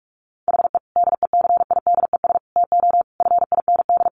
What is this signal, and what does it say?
Morse code, heard as a keyed tone, about 44 wpm, 700 Hz. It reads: HE DECIDES TO FIND
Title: Additional Easy Phrases 45wpm